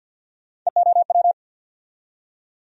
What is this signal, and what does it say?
Morse code, heard as a keyed tone, about 48 wpm, 700 Hz. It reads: EOW
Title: USA Call Sign Postfixes - 3 Letters 50wpm